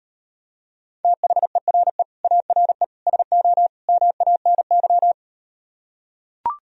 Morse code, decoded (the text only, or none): THERE ARE SO MANY